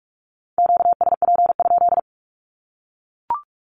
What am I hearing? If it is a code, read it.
Morse code, heard as a keyed tone, about 44 wpm, 700 Hz. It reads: QSP?
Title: Q-Codes 45wpm